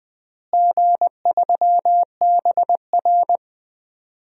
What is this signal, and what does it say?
Morse code, heard as a keyed tone, about 20 wpm, 700 Hz. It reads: G3BR
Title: DXCC-100 Call Signs - 4 Characters - Repeated 3x - 20wpm